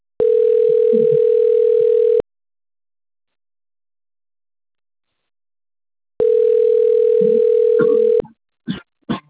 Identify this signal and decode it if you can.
Ringback tone (North America), 440 + 480 Hz.
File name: rg-1600-unknown- - - .2086